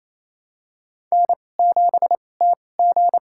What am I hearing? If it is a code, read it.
Morse code, heard as a keyed tone, about 28 wpm, spaced as for 20 wpm, 700 Hz. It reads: N7TG
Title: USA Call Signs - 2x Character Spacing - 28wpm